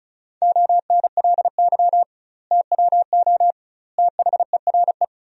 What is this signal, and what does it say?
Morse code, heard as a keyed tone, about 34 wpm, 700 Hz. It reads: ONLY TWO THERE
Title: 2x Word-Spacing - Sentences from Top 200 Words - 35wpm